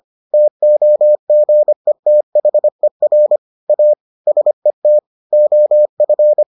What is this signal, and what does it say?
Morse code, heard as a keyed tone, about 25 wpm, 600 Hz. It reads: TOGETHER A SET OF